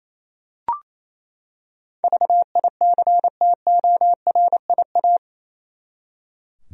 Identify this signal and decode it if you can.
Morse code, heard as a keyed tone, about 28 wpm, 700 Hz. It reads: VICTORIA
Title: USA Common Names - 28wpm